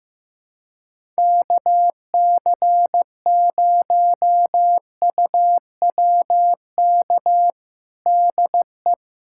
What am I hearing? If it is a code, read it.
Morse code, heard as a keyed tone, about 15 wpm, 700 Hz. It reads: KC0UWK DE
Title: USA Callsign DE Callsign - 6 Characters - 15wpm